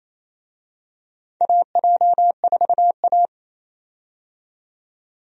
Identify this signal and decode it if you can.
Morse code, heard as a keyed tone, about 28 wpm, 700 Hz. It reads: AJ4A
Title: USA Call Signs - 4 Characters - 28wpm